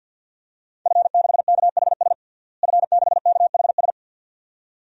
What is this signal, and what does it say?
Morse code, heard as a keyed tone, about 48 wpm, 700 Hz. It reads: F6KHS F6KHS
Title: DXCC-100 Call Signs - 5 Characters - Repeated 2x - 50wpm